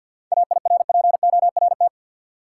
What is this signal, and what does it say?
Morse code, heard as a keyed tone, about 52 wpm, 700 Hz. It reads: AIRPORT
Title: Seven Letter Words - 50wpm